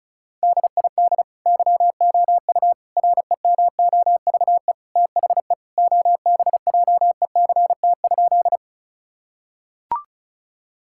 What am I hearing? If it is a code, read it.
Morse code, heard as a keyed tone, about 34 wpm, 700 Hz. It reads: DID YOU REMOVE THE OBJECT?
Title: Sentences from Top 1000 Words - Review Part 5 - 35wpm